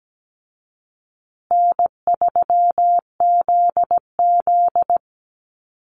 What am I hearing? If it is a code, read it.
Morse code, heard as a keyed tone, about 17 wpm, 700 Hz. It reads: N3ZZ
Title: Easy Call Signs Repeated 2x - 17wpm